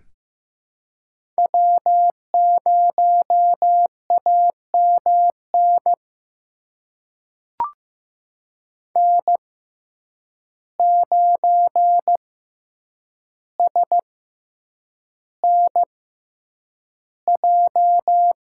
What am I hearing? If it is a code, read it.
Morse code, heard as a keyed tone, about 15 wpm, spaced as for 8 wpm, 700 Hz. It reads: W0AMN N9SNJ
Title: USA Call Signs - 5 Characters - 6x Character Spacing - 15wpm